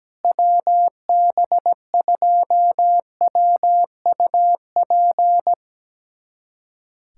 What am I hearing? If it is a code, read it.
Morse code, heard as a keyed tone, about 17 wpm, 700 Hz. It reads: WB2WUP
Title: USA Callsign DE Callsign - 6 Characters - 17wpm